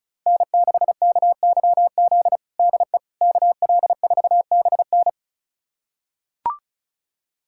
Morse code, heard as a keyed tone, about 36 wpm, 700 Hz. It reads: N6KYZ DE KL4BN
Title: USA Callsign DE Callsign - 5 Characters - 35wpm